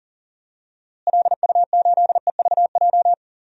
Morse code, heard as a keyed tone, about 40 wpm, 700 Hz. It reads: LU8EVJ
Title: DXCC-100 Call Signs - 6 Characters - Repeated 2x - 40wpm